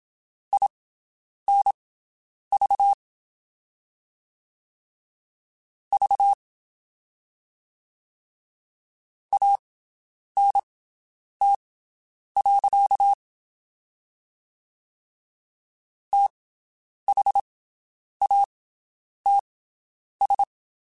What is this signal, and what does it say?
Morse code, heard as a keyed tone, about 27 wpm, spaced as for 9 wpm, 800 Hz. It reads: INV V ANT. THATS
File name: QSOs Spaced